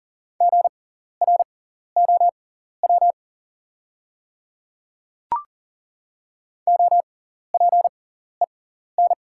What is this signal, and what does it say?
Morse code, heard as a keyed tone, about 40 wpm, spaced as for 14 wpm, 700 Hz. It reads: GROW OPEN
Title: 6x Character Spacing - Top 500 Words - 40wpm